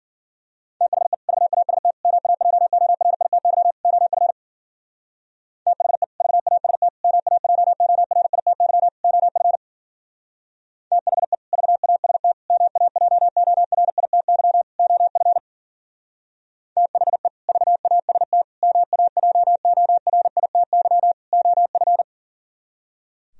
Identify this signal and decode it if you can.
Morse code, heard as a keyed tone, about 52 wpm, 700 Hz. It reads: THE VAST MAJORITY OF THE VAST MAJORITY OF THE VAST MAJORITY OF THE VAST MAJORITY OF